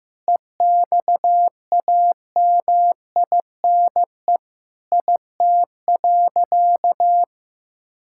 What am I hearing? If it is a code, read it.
Morse code, heard as a keyed tone, about 15 wpm, 700 Hz. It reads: EXAMINE IT.
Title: Sentences from Top 1600 Words 15wpm